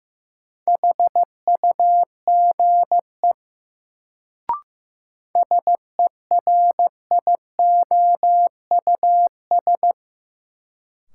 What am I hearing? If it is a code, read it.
Morse code, heard as a keyed tone, about 15 wpm, 700 Hz. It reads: HUGE SERIOUS
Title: Sentences from Top 1100 Words - New Word Drill Alt - 15wpm